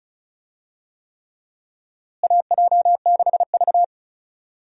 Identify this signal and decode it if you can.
Morse code, heard as a keyed tone, about 36 wpm, 700 Hz. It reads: AJ6V
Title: USA Call Signs - 4 Characters - Repeated 3x - 35wpm